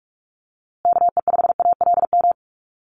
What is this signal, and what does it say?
Morse code, heard as a keyed tone, about 44 wpm, 700 Hz. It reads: KE5ALM
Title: USA Call Signs Repeated 3x 45wpm